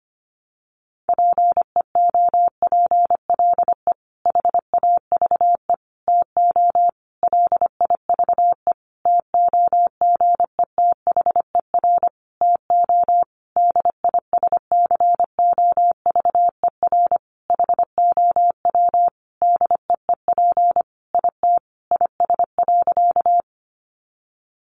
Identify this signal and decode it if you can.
Morse code, heard as a keyed tone, about 25 wpm, 700 Hz. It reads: PEOPLE HAVE TO LIVE TOGETHER TO DISCOVER HOW DEEP IT IS.
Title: Sentences from Top 1100 Words 25wpm